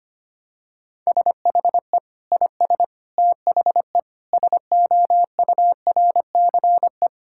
Morse code, heard as a keyed tone, about 25 wpm, 700 Hz. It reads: SHE IS THE SOURCE
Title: Sentences from Top 1000 Words - Review Part 5 - 25wpm